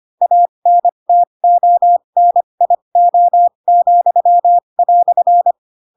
Morse code, heard as a keyed tone, about 25 wpm, spaced as for 22 wpm, 700 Hz. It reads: ANTONIO,"
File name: Pinocchio / Pinocchio ch2 18 25 ik1hge